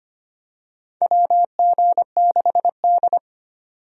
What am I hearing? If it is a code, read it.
Morse code, heard as a keyed tone, about 25 wpm, 700 Hz. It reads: WG6D
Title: USA Call Signs - 4 Characters - 25wpm